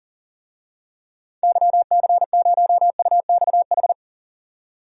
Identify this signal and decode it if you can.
Morse code, heard as a keyed tone, about 40 wpm, 700 Hz. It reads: YC0UXH
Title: DXCC-100 Call Signs - 6 Characters - 40wpm